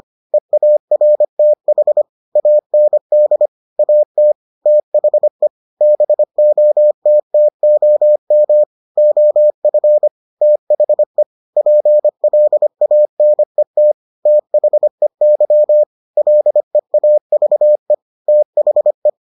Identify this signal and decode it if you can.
Morse code, heard as a keyed tone, about 25 wpm, 600 Hz. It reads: EARTH AND AT THE BOTTOM OF THE PLANET THEY LEAVE THE